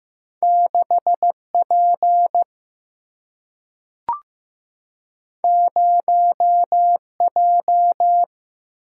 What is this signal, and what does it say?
Morse code, heard as a keyed tone, about 15 wpm, 700 Hz. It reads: 6P 0J
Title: Single-Letter Single-Number 15wpm